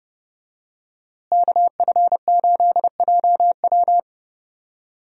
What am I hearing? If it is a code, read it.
Morse code, heard as a keyed tone, about 30 wpm, 700 Hz. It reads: KF8JW
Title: USA Call Signs - 5 Characters - Repeated 2x - 30wpm